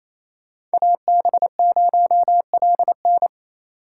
Morse code, heard as a keyed tone, about 28 wpm, 700 Hz. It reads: AB0LN
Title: USA Call Signs - 5 Characters - 4x Character Spacing - 28wpm